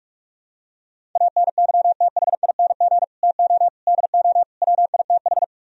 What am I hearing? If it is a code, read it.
Morse code, heard as a keyed tone, about 44 wpm, 700 Hz. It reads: ANYTHING TO DO WITH